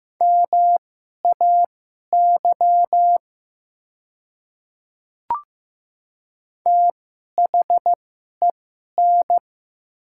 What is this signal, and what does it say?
Morse code, heard as a keyed tone, about 15 wpm, spaced as for 11 wpm, 700 Hz. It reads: MAY THEN
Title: 2x Character Spacing - Top 100 Words - 15wpm